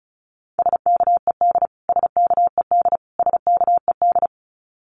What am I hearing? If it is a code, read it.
Morse code, heard as a keyed tone, about 34 wpm, 700 Hz. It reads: SKED SKED SKED
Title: Abbreviations Repeated 3x - 35wpm